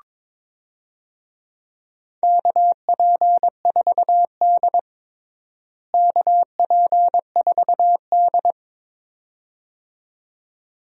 Morse code, heard as a keyed tone, about 22 wpm, 700 Hz. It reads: KP4D KP4D
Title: DXCC-100 Call Signs - 4 Characters - Repeated 2x - 22wpm